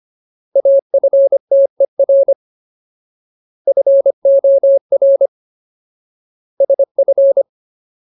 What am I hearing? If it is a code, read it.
Morse code, heard as a keyed tone, about 25 wpm, 550 Hz. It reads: AFTER FOR SF
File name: Ultimate Word List / 25WPM Ultimate